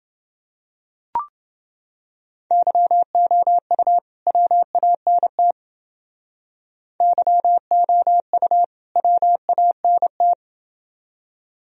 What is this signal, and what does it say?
Morse code, heard as a keyed tone, about 29 wpm, 700 Hz. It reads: YOU WANT YOU WANT